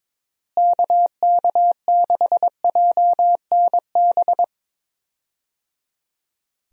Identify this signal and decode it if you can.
Morse code, heard as a keyed tone, about 22 wpm, 700 Hz. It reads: KK6JNB
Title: USA Call Signs - 6 Characters - Repeated 3x - 22wpm